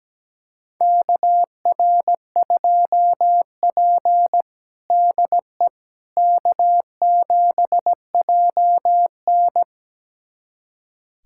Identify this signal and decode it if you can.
Morse code, heard as a keyed tone, about 17 wpm, 700 Hz. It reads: KR2P DE K7JN
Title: USA Callsign DE Callsign - 4 Characters - 17wpm